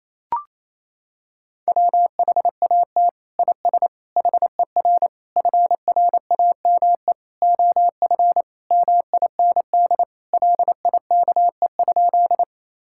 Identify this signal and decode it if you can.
Morse code, heard as a keyed tone, about 27 wpm, 700 Hz. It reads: WHAT IS HER FRAME OF MIND LIKE?
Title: Sentences from Top 1000 Words - Review Part 6 - 28wpm